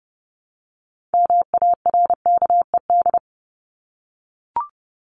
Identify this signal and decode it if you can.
Morse code, heard as a keyed tone, about 30 wpm, 700 Hz. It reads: MARKED
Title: ED Suffix - Top 100 Words with Base Words 30wpm